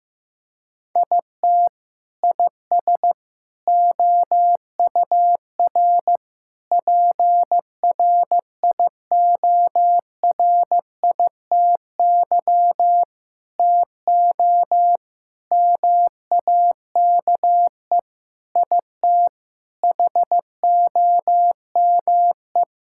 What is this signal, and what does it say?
Morse code, heard as a keyed tone, about 15 wpm, 700 Hz. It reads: IT IS OUR PRIORITY TO MAKE IT HOME
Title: Sentences from Top 1600 Words 15wpm